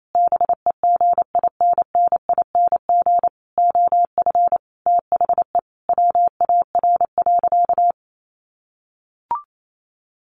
Morse code, heard as a keyed tone, about 28 wpm, 700 Hz. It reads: BEGINNING OF THE WAR.
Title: Sentences from Top 1400 Words 28wpm